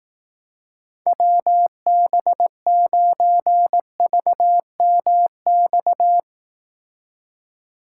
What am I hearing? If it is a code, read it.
Morse code, heard as a keyed tone, about 18 wpm, 700 Hz. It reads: WB9VMX
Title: Speed Racing - USA 2x3 Call Signs 15wpm